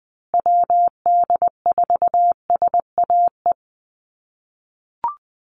Morse code, heard as a keyed tone, about 20 wpm, 700 Hz. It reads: WD4SAE
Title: USA Call Signs Repeated 3x 20wpm